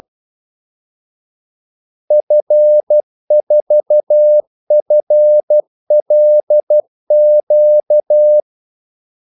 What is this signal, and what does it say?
Morse code, heard as a keyed tone, about 12 wpm, 600 Hz. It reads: F4FLQ
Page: 2 ekstra ordmellemrum mellem kaldesignalerne